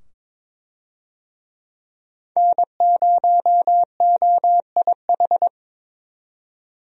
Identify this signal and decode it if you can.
Morse code, heard as a keyed tone, about 22 wpm, 700 Hz. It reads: N0OIH
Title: USA Call Signs - 5 Characters - 22wpm